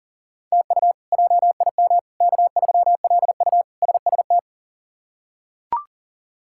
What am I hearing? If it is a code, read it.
Morse code, heard as a keyed tone, about 40 wpm, 700 Hz. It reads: TU JIM K3LU SST
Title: K1USN SST QSO Practice - 40wpm